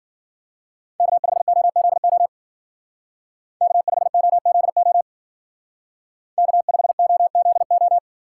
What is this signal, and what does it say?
Morse code, heard as a keyed tone, about 48 wpm, 700 Hz. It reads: K5OZO K5OZO K5OZO
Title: Speed Racing - USA 1x3 Call Signs 40wpm